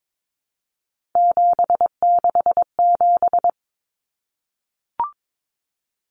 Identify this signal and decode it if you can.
Morse code, heard as a keyed tone, about 22 wpm, 700 Hz. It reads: 767
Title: Numbers - 3-Digit 22wpm